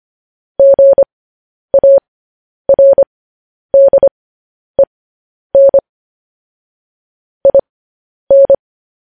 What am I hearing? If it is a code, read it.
Morse code, heard as a keyed tone, about 25 wpm, spaced as for 10 wpm, 550 Hz. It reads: GARDEN IN
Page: news headlines in morse code at 10 wpm